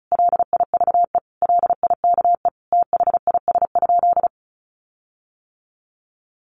Morse code, heard as a keyed tone, about 34 wpm, 700 Hz. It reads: LIVE LIKE THIS?
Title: Sentences from Top 1400 Words 35wpm